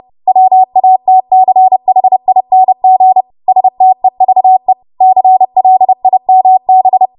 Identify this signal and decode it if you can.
Morse code, heard as a keyed tone, about 30 wpm, 750 Hz. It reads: WATCHING STEVE CLIMB
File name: Morse / 30WPM